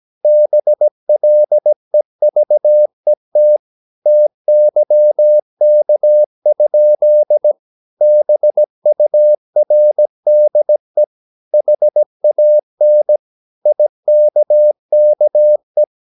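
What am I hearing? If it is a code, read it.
Morse code, heard as a keyed tone, about 17 wpm, 600 Hz. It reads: BLEVET TYK? BURDE HAN IKKE
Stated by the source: Slow (QRS) Morse practice at 17 wpm, from CW med Gnister.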